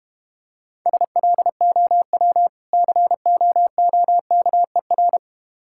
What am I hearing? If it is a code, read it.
Morse code, heard as a keyed tone, about 32 wpm, 700 Hz. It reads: SLOW COOKER